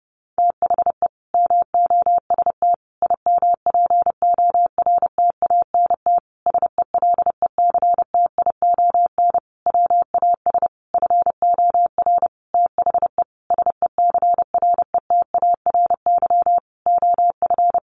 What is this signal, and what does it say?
Morse code, heard as a keyed tone, about 30 wpm, 700 Hz. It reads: THE MOST IMPORTANT SELECTION WAS FOR THE SECRETARY OF
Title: Sentences from Top 1400 Words 30wpm